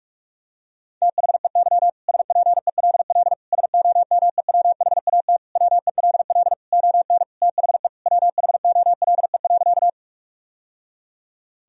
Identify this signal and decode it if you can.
Morse code, heard as a keyed tone, about 46 wpm, 700 Hz. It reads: THEY SPELL SOMEWHAT WELL ON THE WHOLE.
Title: Sentences from Top 1000 Words - Review Part 1 - 45wpm